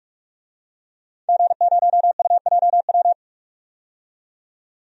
Morse code, heard as a keyed tone, about 46 wpm, 700 Hz. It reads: G0UJW
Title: DXCC-100 Call Signs - 5 Characters - 45wpm